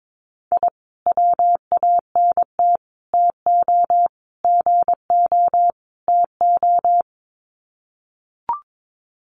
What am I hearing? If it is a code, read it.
Morse code, heard as a keyed tone, about 22 wpm, 700 Hz. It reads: I WANT TO GO TO
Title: Sets of 5 Words Encore 22wpm